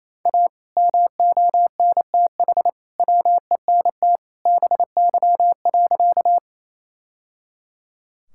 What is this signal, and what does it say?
Morse code, heard as a keyed tone, about 28 wpm, 700 Hz. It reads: A MONTH WENT BY.
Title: Sentences from Top 1000 Words - Review Part 5 - 28wpm